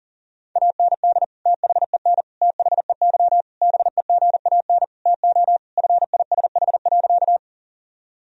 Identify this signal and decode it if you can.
Morse code, heard as a keyed tone, about 40 wpm, 700 Hz. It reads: AND THEN THEY BEGAN TO FISH.
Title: Sentences from Top 1000 Words - Review Part 1 - 40wpm